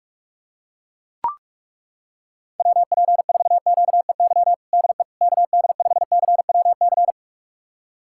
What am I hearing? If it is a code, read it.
Morse code, heard as a keyed tone, about 42 wpm, 700 Hz. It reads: WP4QEY DE KD5CWC
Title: USA Callsign DE Callsign 45wpm